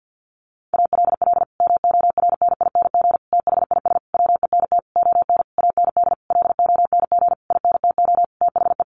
Morse code, heard as a keyed tone, about 50 wpm, 700 Hz. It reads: ALL MORNING THIS WENT ON AND LONG INTO THE
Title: Sentences from Top 1500 Words 50wpm